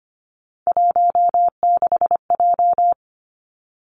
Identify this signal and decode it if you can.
Morse code, heard as a keyed tone, about 25 wpm, 700 Hz. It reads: 16J
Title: Three Challenging Letter-Number Combinations 25wpm